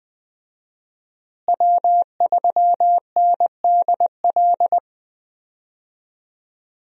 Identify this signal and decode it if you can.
Morse code, heard as a keyed tone, about 20 wpm, 700 Hz. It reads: W3NDL
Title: USA Call Signs - 5 Characters - Repeated 3x - 20wpm